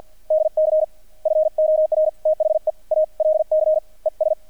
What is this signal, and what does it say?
Morse code, heard as a keyed tone, about 48 wpm, 620 Hz. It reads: DC FOR THE ALC ES
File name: archived / CRSnet